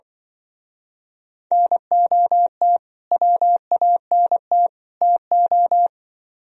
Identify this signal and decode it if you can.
Morse code, heard as a keyed tone, about 24 wpm, 700 Hz. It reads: NOT WANT TO
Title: Speed-Racing - Sets of 3 Words 20wpm